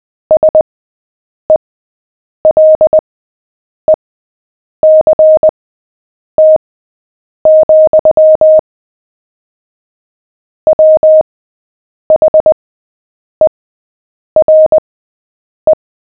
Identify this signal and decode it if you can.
Morse code, heard as a keyed tone, about 20 wpm, spaced as for 8 wpm, 620 Hz. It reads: SELECT, WHERE
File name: cw / fists / Short-Story-7-8wpm